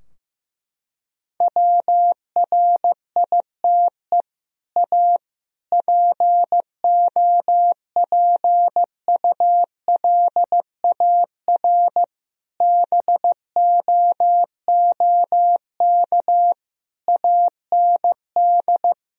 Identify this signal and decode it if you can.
Morse code, heard as a keyed tone, about 15 wpm, 700 Hz. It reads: WRITE A POPULAR BOOK AND
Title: Sentences from Top 1000 Words - Review Part 3 - 15wpm